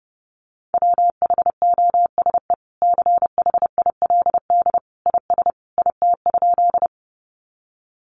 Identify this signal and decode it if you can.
Morse code, heard as a keyed tone, about 30 wpm, 700 Hz. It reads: WHOSE CHILD IS IT?
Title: Sentences from Top 1200 Words 30wpm